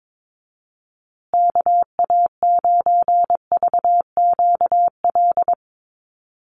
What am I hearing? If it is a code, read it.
Morse code, heard as a keyed tone, about 22 wpm, 700 Hz. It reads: KA9VQL
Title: USA Call Signs Repeated 3x 22wpm